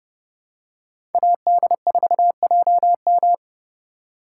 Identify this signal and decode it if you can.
Morse code, heard as a keyed tone, about 29 wpm, 700 Hz. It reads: AD4JM